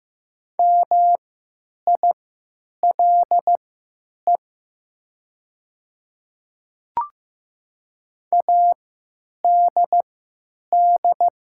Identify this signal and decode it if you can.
Morse code, heard as a keyed tone, about 15 wpm, spaced as for 9 wpm, 700 Hz. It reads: MILE ADD
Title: 3x Character Spacing - Top 300 Words - 15wpm